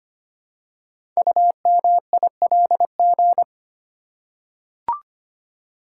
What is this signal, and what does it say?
Morse code, heard as a keyed tone, about 25 wpm, 700 Hz. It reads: UMILG
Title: Code Groups - 5 Letters - 25wpm